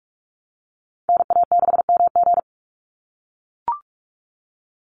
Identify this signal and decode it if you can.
Morse code, heard as a keyed tone, about 44 wpm, 700 Hz. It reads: NA6MG
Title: NA SOTA Chaser Callsigns 45wpm